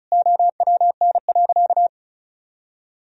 Morse code, heard as a keyed tone, about 36 wpm, 700 Hz. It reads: OWN.
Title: Sentences from Top 1000 Words - Review Part 4 - 35wpm